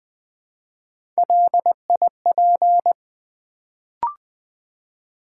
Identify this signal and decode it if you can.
Morse code, heard as a keyed tone, about 20 wpm, 700 Hz. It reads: LIP
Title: Build A Word - 1 Letter At A Time - 3-Letter Words - 20wpm